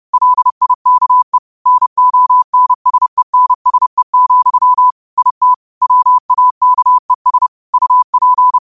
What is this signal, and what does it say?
Morse code, heard as a keyed tone, about 30 wpm, 1000 Hz. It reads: LIKE NONSENSE, IT WAKES UP
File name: qotdfiles / 30wpm